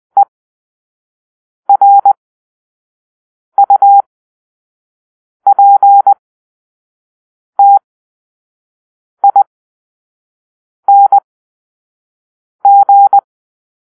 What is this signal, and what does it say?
Morse code, heard as a keyed tone, about 20 wpm, spaced as for 5 wpm, 800 Hz. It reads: ERUPTING
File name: arrl / Grp-5 Drill-4